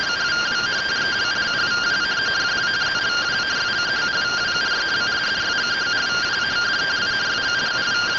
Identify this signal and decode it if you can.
Two-tone teleprinter signal, about 170 Hz shift.